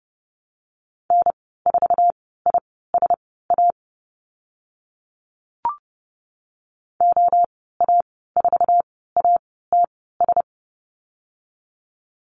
Morse code, heard as a keyed tone, about 30 wpm, spaced as for 17 wpm, 700 Hz. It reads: N4ISA OA4ATS
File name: Lesson 15g - Number 5 - Call Signs 30wpm